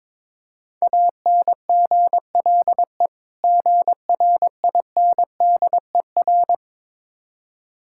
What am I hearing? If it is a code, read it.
Morse code, heard as a keyed tone, about 23 wpm, 690 Hz. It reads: ANGLE GRINDER
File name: Difficult Words with Hints - 22wpm